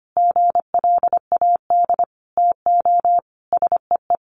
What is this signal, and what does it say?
Morse code, heard as a keyed tone, about 25 wpm, 700 Hz. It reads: GLAD TO SEE
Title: Sentences from Top 800 Words 25wpm